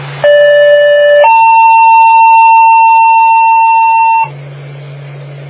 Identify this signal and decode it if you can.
Two-tone page, heard as a one-second tone, then a three-second tone, about 580 then 900 Hz. It is heard from 0.2 to 4.2 s.